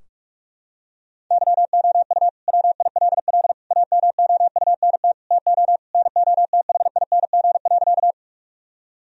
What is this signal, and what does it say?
Morse code, heard as a keyed tone, about 46 wpm, 700 Hz. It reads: YOU WILL AMOUNT TO NOTHING.
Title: Sentences from Top 1000 Words - Review Part 4 - 45wpm